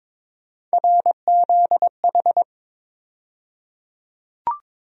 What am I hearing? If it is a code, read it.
Morse code, heard as a keyed tone, about 22 wpm, 700 Hz. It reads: RZH
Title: USA Call Sign Postfixes - 3 Letters 22wpm